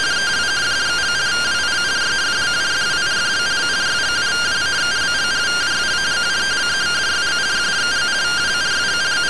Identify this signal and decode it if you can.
Two-tone teleprinter signal, about 170 Hz shift.